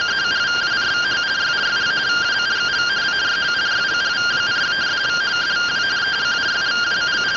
Two-tone teleprinter signal, about 170 Hz shift.